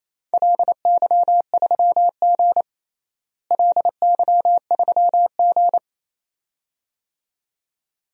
Morse code, heard as a keyed tone, about 28 wpm, 700 Hz. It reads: LY3G LY3G
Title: DXCC-100 Call Signs - 4 Characters - Repeated 2x - 28wpm